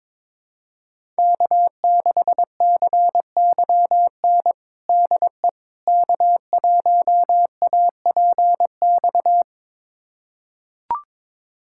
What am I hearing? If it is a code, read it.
Morse code, heard as a keyed tone, about 22 wpm, 700 Hz. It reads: K6CYN DE K1APX
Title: USA Callsign DE Callsign - 5 Characters - 22wpm